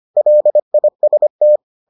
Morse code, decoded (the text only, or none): LIST